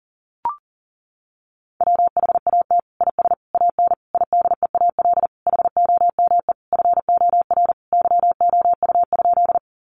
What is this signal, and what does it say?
Morse code, heard as a keyed tone, about 40 wpm, 700 Hz. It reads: WHAT IS AN IDEAL HOME FOR YOU?
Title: Sentences from Top 1400 Words 40wpm